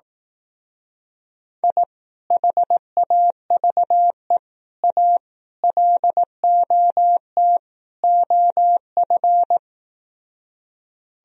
Morse code, heard as a keyed tone, about 18 wpm, 700 Hz. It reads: I HAVE A LOT OF
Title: Speed-Racing - Sets of 5 Words 15wpm